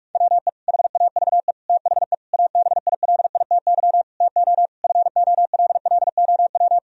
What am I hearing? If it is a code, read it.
Morse code, heard as a keyed tone, about 41 wpm, 700 Hz. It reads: WE HAVE THE ABILITY TO FOLLOW
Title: Sentences from Top 1000 Words - Review Part 3 - 45wpm